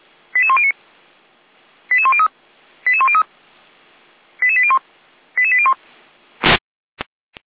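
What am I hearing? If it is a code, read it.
Five-tone selective call (ZVEI), heard as quick runs of five tones, tones of about 70 ms. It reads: 88109 88103 88103 88801 88801
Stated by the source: ZVEI five-tone selective call 88402, received by SDR, Feuerwehr Orlamünde.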